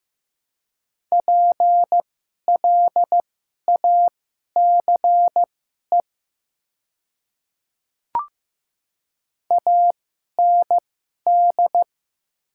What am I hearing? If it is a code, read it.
Morse code, heard as a keyed tone, about 15 wpm, spaced as for 11 wpm, 700 Hz. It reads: PLACE AND
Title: 2x Character Spacing - Top 300 Words - 15wpm